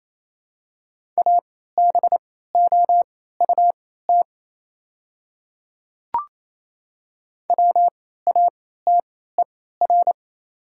Morse code, heard as a keyed tone, about 28 wpm, spaced as for 16 wpm, 700 Hz. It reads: ABOUT WATER
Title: 3x Character Spacing - Top 100 Words - 28wpm